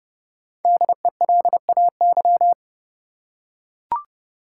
Morse code, heard as a keyed tone, about 30 wpm, 700 Hz. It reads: DELAY
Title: Five Letter Words - 30wpm